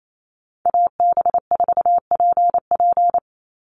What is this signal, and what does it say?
Morse code, heard as a keyed tone, about 28 wpm, 700 Hz. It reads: AB4PP
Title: NA SOTA Chaser Callsigns 28wpm